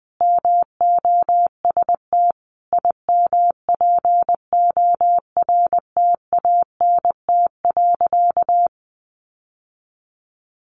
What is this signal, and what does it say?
Morse code, heard as a keyed tone, about 20 wpm, 700 Hz. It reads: MOST IMPORTANT.
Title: Sentences from Top 700 Words 20wpm